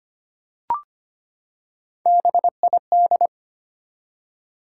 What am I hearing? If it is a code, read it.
Morse code, heard as a keyed tone, about 25 wpm, 700 Hz. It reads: BID
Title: Three Letter Words - 25wpm